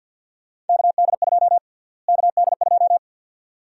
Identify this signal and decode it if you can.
Morse code, heard as a keyed tone, about 51 wpm, 700 Hz. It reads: KD2 KD2
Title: USA Call Sign Prefixes - Repeated 2x - 50wpm